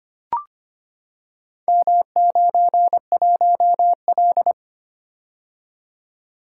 Morse code, heard as a keyed tone, about 25 wpm, 690 Hz. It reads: M91L
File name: practice / 25wpm / Four Letter-Number Combinations - 25wpm